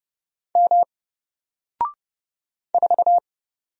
Morse code, heard as a keyed tone, about 30 wpm, 700 Hz. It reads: M 4
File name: Single-Letter-Number-30wpm